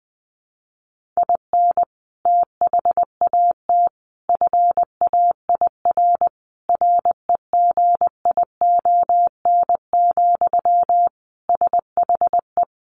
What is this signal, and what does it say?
Morse code, heard as a keyed tone, about 20 wpm, 700 Hz. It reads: IN THAT FAIR REGION, SHE
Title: Sentences from Top 1500 Words 20wpm